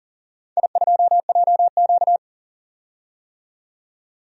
Morse code, heard as a keyed tone, about 40 wpm, 700 Hz. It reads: I2JQ
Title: DXCC-100 Call Signs - 4 Characters - 40wpm